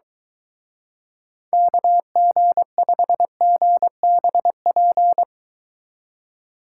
Morse code, heard as a keyed tone, about 22 wpm, 700 Hz. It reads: KG5GBP